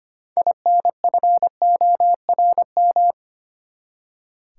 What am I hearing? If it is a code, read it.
Morse code, heard as a keyed tone, about 25 wpm, 700 Hz. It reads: INFORM
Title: Six Letter Words - 25wpm